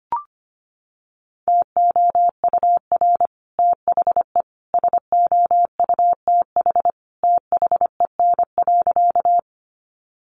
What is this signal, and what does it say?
Morse code, heard as a keyed tone, about 25 wpm, 700 Hz. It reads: TOUR THE SOUTH THEN.
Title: Sentences from Top 1400 Words 25wpm